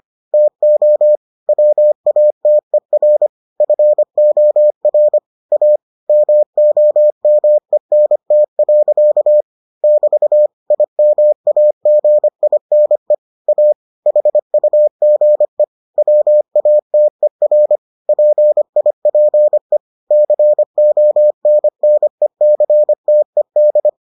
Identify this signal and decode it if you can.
Morse code, heard as a keyed tone, about 25 wpm, 600 Hz. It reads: TO WATER FOR A MOMENT. <BT> IMAGINE A HUGE WATER PIPE CONNECTED